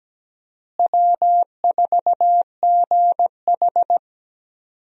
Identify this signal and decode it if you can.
Morse code, heard as a keyed tone, about 17 wpm, 700 Hz. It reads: W4GH
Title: USA Call Signs - 4 Characters - Repeated 3x - 17wpm